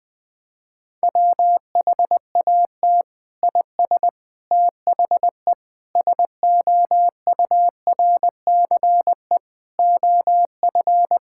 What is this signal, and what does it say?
Morse code, heard as a keyed tone, about 20 wpm, 700 Hz. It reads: WHAT IS THE SOURCE OF
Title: Sentences from Top 1000 Words - Review Part 5 - 20wpm